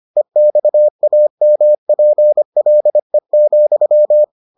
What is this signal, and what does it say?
Morse code, heard as a keyed tone, about 25 wpm, 600 Hz. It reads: EXAMPLE,